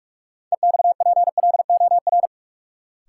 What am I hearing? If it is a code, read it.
Morse code, heard as a keyed tone, about 47 wpm, 700 Hz. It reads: EXPLOR
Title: Sentences from Top 1100 Words - New Word Drill - 45wpm